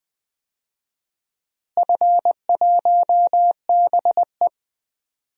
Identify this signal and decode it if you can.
Morse code, heard as a keyed tone, about 20 wpm, 700 Hz. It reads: F1BE
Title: DXCC-100 Call Signs - 4 Characters - Repeated 2x - 20wpm